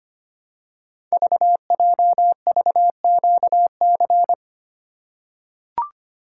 Morse code, heard as a keyed tone, about 25 wpm, 700 Hz. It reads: VJVQC
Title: Code Groups - 5 Letters - 25wpm